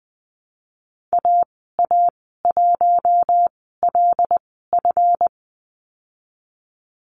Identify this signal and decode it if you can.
Morse code, heard as a keyed tone, about 20 wpm, spaced as for 15 wpm, 700 Hz. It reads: AA1LF
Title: 186 Lesson 30g - Letter b - Call Signs 20wpm